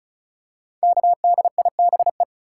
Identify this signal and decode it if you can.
Morse code, heard as a keyed tone, about 36 wpm, 700 Hz. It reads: KDIBE
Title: Code Groups - 5 Letters - 35wpm